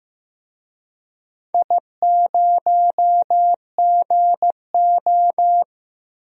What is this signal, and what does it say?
Morse code, heard as a keyed tone, about 15 wpm, 700 Hz. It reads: I0GO
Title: DXCC-100 Call Signs - 4 Characters - 15wpm